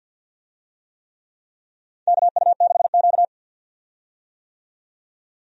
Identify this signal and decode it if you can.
Morse code, heard as a keyed tone, about 45 wpm, 700 Hz. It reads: KU6Q
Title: USA Call Signs - 4 Characters - 50wpm